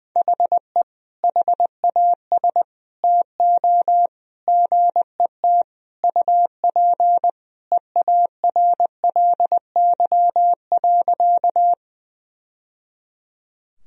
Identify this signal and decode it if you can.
Morse code, heard as a keyed tone, about 20 wpm, 700 Hz. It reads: HE HAS TO GET UP EARLY.
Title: Sentences from Top 1000 Words - Review Part 2 - 20wpm